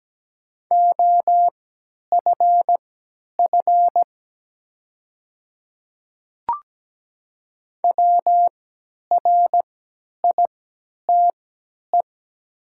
Morse code, heard as a keyed tone, about 17 wpm, spaced as for 10 wpm, 700 Hz. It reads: OFF WRITE
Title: 3x Character Spacing - Top 200 Words - 17wpm